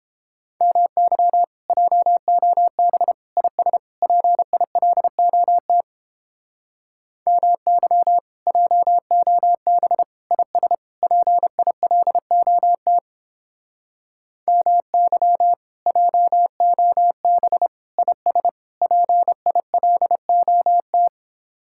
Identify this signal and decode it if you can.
Morse code, heard as a keyed tone, about 29 wpm, 700 Hz. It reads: MY JOB IS PILOT MY JOB IS PILOT MY JOB IS PILOT